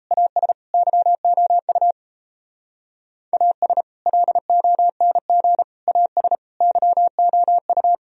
Morse code, heard as a keyed tone, about 34 wpm, 700 Hz. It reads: AS YOU AS LONG AS YOU
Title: Speed-Racing - Sets of 4 Words 28wpm